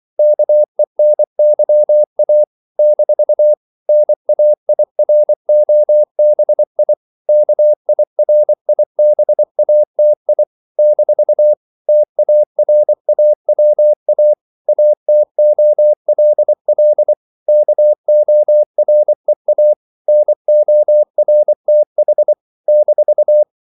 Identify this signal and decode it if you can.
Morse code, heard as a keyed tone, about 24 wpm, 600 Hz. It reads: KENYA - NAIROBI KIRIBATI - TARAWA ATOLL KOREA NORTH -
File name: CW / Practice / NationalCapitals24